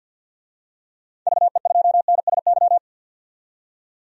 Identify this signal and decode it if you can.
Morse code, heard as a keyed tone, about 52 wpm, 700 Hz. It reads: VE2NSY
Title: DXCC-100 Call Signs - 6 Characters - 50wpm